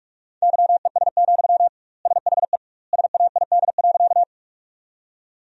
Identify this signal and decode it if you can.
Morse code, heard as a keyed tone, about 46 wpm, 700 Hz. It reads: YES, SHE SAID.
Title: 2x Word-Spacing - Sentences from Top 500 Words - 45wpm